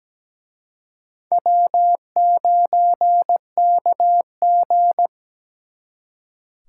Morse code, heard as a keyed tone, about 20 wpm, 700 Hz. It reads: W9KG
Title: Speed Racing - USA 1x2 Call Signs 17wpm